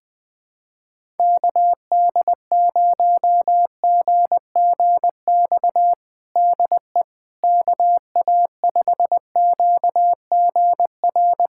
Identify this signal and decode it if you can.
Morse code, heard as a keyed tone, about 21 wpm, 700 Hz. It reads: KD0GGX DE KA5QGR
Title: USA Callsign DE Callsign 20wpm